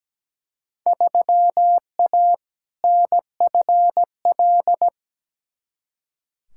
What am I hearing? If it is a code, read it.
Morse code, heard as a keyed tone, about 17 wpm, 700 Hz. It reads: 3A NFL
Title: Field Day Exchanges - 17wpm